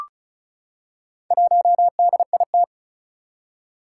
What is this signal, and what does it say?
Morse code, heard as a keyed tone, about 36 wpm, 700 Hz. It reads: 1DIT
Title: Four Challenging Letter-Number Combinations 35wpm